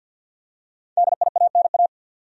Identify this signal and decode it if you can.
Morse code, heard as a keyed tone, about 53 wpm, 700 Hz. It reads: DIANA
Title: USA Common Names - Max 5 Letters - 50wpm